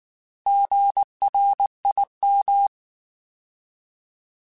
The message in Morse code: GRIM

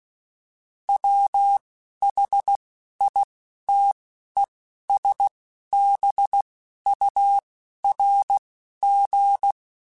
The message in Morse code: WHITESBURG

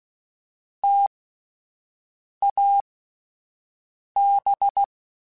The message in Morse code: TAB

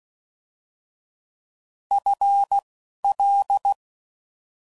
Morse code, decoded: F L

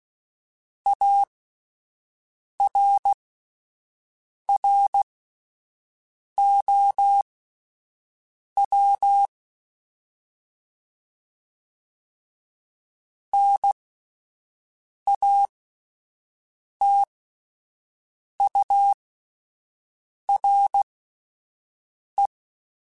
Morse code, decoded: ARROW NATURE

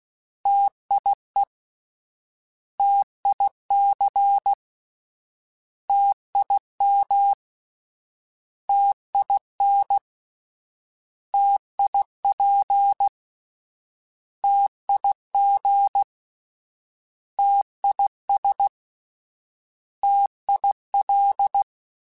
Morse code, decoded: TIE TIC TIM TIN TIP TIG TIS TIL

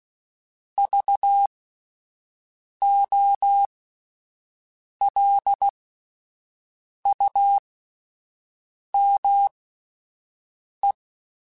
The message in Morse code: VOLUME

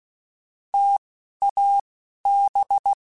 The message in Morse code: TAB

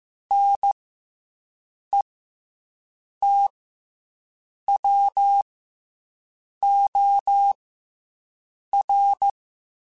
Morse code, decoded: NETWOR

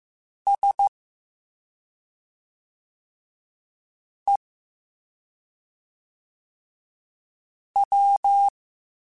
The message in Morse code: SEW

